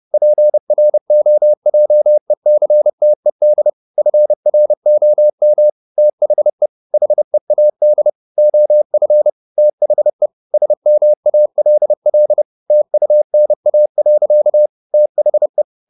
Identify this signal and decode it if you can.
Morse code, heard as a keyed tone, about 30 wpm, 600 Hz. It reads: PROJECTED FROM THE HEAD OF THE SMALL TUNA. THE
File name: the old man and the sea / the old man and the sea 1 30wpm